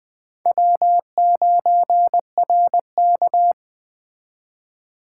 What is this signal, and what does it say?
Morse code, heard as a keyed tone, about 21 wpm, 700 Hz. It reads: W9RK
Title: Speed Racing - USA 1x2 Call Signs 17wpm